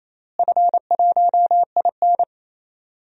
Morse code, heard as a keyed tone, about 28 wpm, 700 Hz. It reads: F1IN
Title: DXCC-100 Call Signs - 4 Characters - Repeated 2x - 28wpm